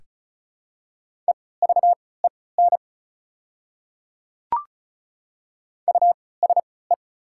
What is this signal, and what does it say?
Morse code, heard as a keyed tone, about 36 wpm, spaced as for 20 wpm, 700 Hz. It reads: EVEN USE
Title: 3x Character Spacing - Top 500 Words - 35wpm